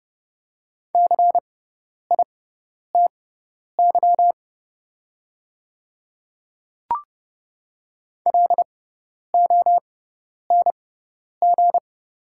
Morse code, heard as a keyed tone, about 30 wpm, spaced as for 10 wpm, 700 Hz. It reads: CITY LONG
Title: 6x Character Spacing - Top 300 Words - 30wpm